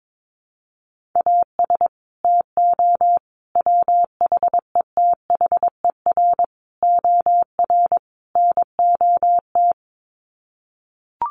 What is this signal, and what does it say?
Morse code, heard as a keyed tone, about 22 wpm, 700 Hz. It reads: AS TO WHETHER OR NOT
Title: Sets of 5 Words Encore 22wpm